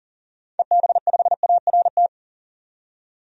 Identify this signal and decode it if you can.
Morse code, heard as a keyed tone, about 40 wpm, 700 Hz. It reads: EB5ART
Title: DXCC-100 Call Signs - 6 Characters - 40wpm